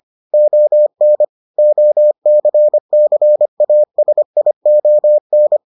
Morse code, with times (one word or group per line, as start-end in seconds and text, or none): 0.3-1.2 ON
1.6-5.6 OCCASION